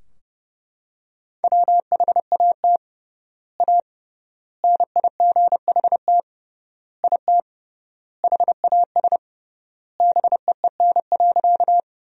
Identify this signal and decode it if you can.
Morse code, heard as a keyed tone, about 30 wpm, 700 Hz. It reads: WHAT A NIGHT IT HAS BEEN.
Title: 3x Word-Spacing - Sentences from Top 300 Words - 30wpm